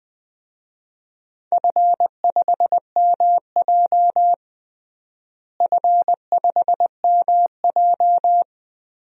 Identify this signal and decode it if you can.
Morse code, heard as a keyed tone, about 20 wpm, 700 Hz. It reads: F5MJ F5MJ
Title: DXCC-100 Call Signs - 4 Characters - Repeated 2x - 20wpm